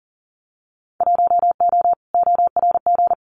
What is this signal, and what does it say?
Morse code, heard as a keyed tone, about 40 wpm, 700 Hz. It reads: 1O ORG
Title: Winter Field Day Exchanges 40wpm